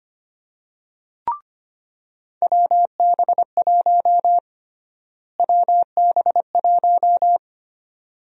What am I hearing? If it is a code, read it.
Morse code, heard as a keyed tone, about 25 wpm, 700 Hz. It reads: WB1 WB1
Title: USA Call Sign Prefixes - Repeated 2x - 25wpm